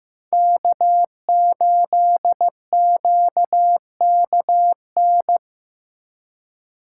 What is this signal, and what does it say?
Morse code, heard as a keyed tone, about 18 wpm, 700 Hz. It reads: K8QKN
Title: Speed Racing - USA Call Signs 15wpm